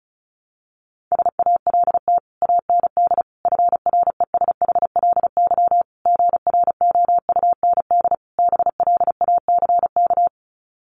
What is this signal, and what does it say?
Morse code, heard as a keyed tone, about 34 wpm, 700 Hz. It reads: SALT AND FRESHLY GROUND BLACK
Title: Sets of 5 Words Encore 35wpm